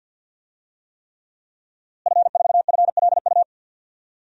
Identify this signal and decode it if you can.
Morse code, heard as a keyed tone, about 53 wpm, 700 Hz. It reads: F4FLU
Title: DXCC-100 Call Signs - 5 Characters - Repeated 3x - 50wpm